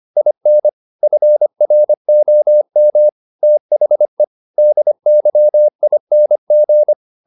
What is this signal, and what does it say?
Morse code, heard as a keyed tone, about 25 wpm, 600 Hz. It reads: IN FROM THE DYING
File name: the old man and the sea part 1 25wpm